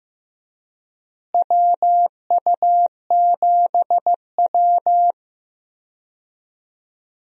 Morse code, heard as a keyed tone, about 15 wpm, 700 Hz. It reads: WU7W
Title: USA Call Signs - 4 Characters - Repeated 2x - 15wpm